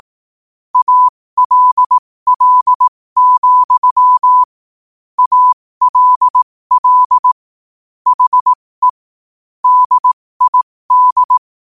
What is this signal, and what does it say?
Morse code, heard as a keyed tone, about 18 wpm, spaced as for 15 wpm, 1000 Hz. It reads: ALL, ALL HE DID
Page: New quotes every day in morse code at 15 Words per minute.